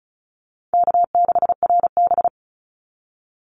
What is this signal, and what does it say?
Morse code, heard as a keyed tone, about 34 wpm, 700 Hz. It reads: K6RB
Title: Call Signs Repeated 2x - 35wpm